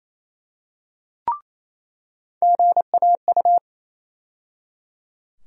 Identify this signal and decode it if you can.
Morse code, heard as a keyed tone, about 28 wpm, 700 Hz. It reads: GAU
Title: USA Call Sign Postfixes - 3 Letters 28wpm